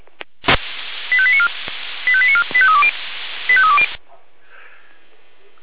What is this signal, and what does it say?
Five-tone selective call (ZVEI), heard as quick runs of five tones, tones of about 70 ms. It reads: 85894 85894 85320 85320